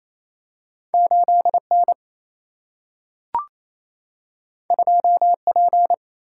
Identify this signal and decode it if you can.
Morse code, heard as a keyed tone, about 28 wpm, 700 Hz. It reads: 8 N 2 P
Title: Single-Letter Single-Number 28wpm